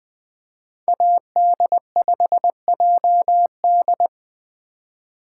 Morse code, heard as a keyed tone, about 20 wpm, 700 Hz. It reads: AD5JD
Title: USA Call Signs - 5 Characters - 4x Character Spacing - 20wpm